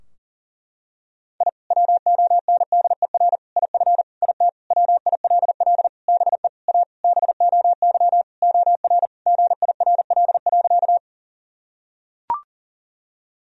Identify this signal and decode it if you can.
Morse code, heard as a keyed tone, about 40 wpm, 700 Hz. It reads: I WONDER IF IT WILL BE A BOY OR GIRL.
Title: Sentences from Top 1000 Words - Review Part 3 - 40wpm